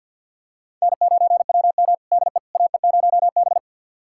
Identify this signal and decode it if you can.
Morse code, heard as a keyed tone, about 50 wpm, 700 Hz. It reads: N9WM DE AE0B
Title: USA Callsign DE Callsign - 4 Characters - 50wpm